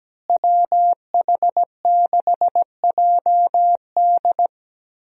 Morse code, heard as a keyed tone, about 17 wpm, 700 Hz. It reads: WH6JD
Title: USA Call Signs - 5 Characters - Repeated 3x - 17wpm